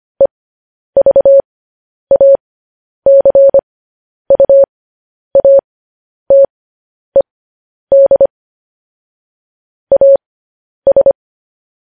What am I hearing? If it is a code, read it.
Morse code, heard as a keyed tone, about 25 wpm, spaced as for 10 wpm, 550 Hz. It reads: EVACUATED AS
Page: news headlines in morse code at 10 wpm